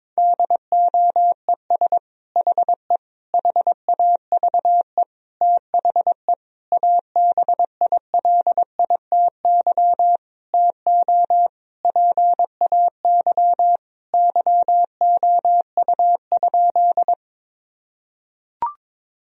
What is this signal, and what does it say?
Morse code, heard as a keyed tone, about 21 wpm, 700 Hz. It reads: DOES HE HAVE THE ABILITY TO PAY YOU?
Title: Sentences from Top 1000 Words - Review Part 3 - 22wpm